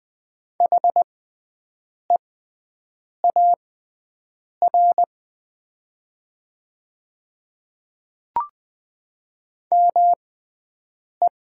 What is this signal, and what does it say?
Morse code, heard as a keyed tone, about 20 wpm, spaced as for 7 wpm, 700 Hz. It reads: HEAR ME
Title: 6x Character Spacing - Top 500 Words - 20wpm